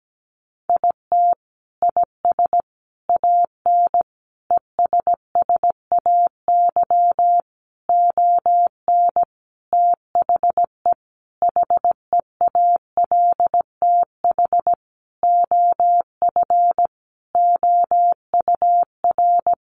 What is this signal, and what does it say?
Morse code, heard as a keyed tone, about 17 wpm, 700 Hz. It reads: IT IS AN ESSAY ON THE HEALTH OF OUR
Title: Sentences from Top 1500 Words 17wpm